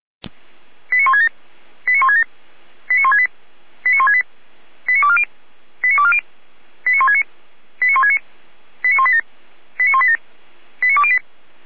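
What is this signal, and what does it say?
Five-tone selective call (ZVEI), heard as quick runs of five tones, tones of about 70 ms. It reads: 89167 89167 89168 89168 89240 89240 89169 89169 89187 89187 89208
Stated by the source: ZVEI five-tone call received on SDR, from audio server of feuerwehr-orlamuende (fire brigade Orlamünde).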